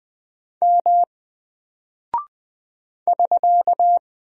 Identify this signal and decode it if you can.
Morse code, heard as a keyed tone, about 20 wpm, 700 Hz. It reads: M <SK>
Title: Single Character (Letter:Number:Punctuation:Prosign) Alt 20wpm